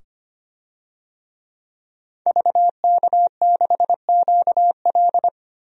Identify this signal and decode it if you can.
Morse code, heard as a keyed tone, about 25 wpm, 700 Hz. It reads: VK6QL
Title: DXCC-100 Call Signs - 5 Characters - 25wpm